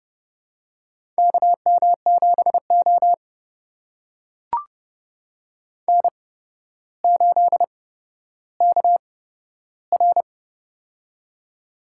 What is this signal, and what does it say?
Morse code, heard as a keyed tone, about 30 wpm, spaced as for 13 wpm, 700 Hz. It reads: KM7O N8KR
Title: USA Call Signs - 4 Characters - 8x Character Spacing - 30wpm